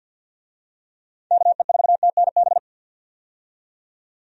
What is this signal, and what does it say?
Morse code, heard as a keyed tone, about 51 wpm, 700 Hz. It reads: KE4TNB
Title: USA Call Signs - 6 Characters - 50wpm